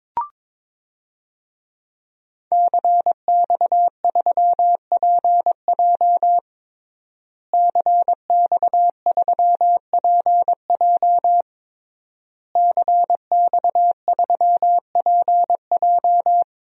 Morse code, heard as a keyed tone, about 22 wpm, 700 Hz. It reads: CX3PJ CX3PJ CX3PJ
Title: DXCC-100 Call Signs - 5 Characters - Repeated 3x - 22wpm